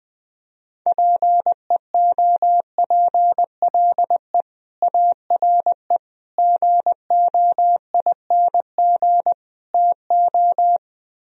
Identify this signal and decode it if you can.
Morse code, heard as a keyed tone, about 20 wpm, 700 Hz. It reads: PEOPLE ARE GOING TO